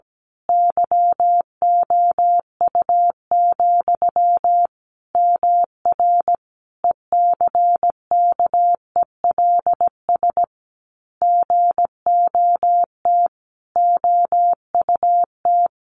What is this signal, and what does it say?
Morse code, heard as a keyed tone, about 17 wpm, 700 Hz. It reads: YOU, MR ECKELS GOT OUT
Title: A Sound Of Thunder 17wpm